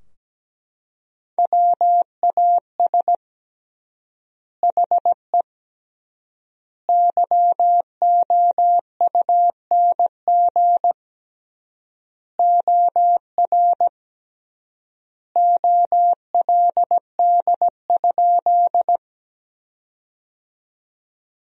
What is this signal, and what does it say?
Morse code, heard as a keyed tone, about 17 wpm, 700 Hz. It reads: WAS HE YOUNG OR OLD?
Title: 3x Word-Spacing - Sentences from Top 400 Words - 17wpm